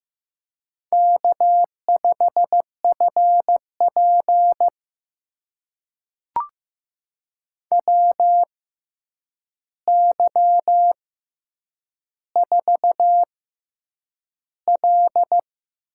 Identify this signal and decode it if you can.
Morse code, heard as a keyed tone, about 15 wpm, spaced as for 8 wpm, 700 Hz. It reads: K5FP WY4L
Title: USA Call Signs - 4 Characters - 6x Character Spacing - 15wpm